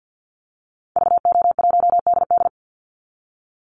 Morse code, heard as a keyed tone, about 49 wpm, 700 Hz. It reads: VO1DD
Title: Call Signs Repeated 3x - 50wpm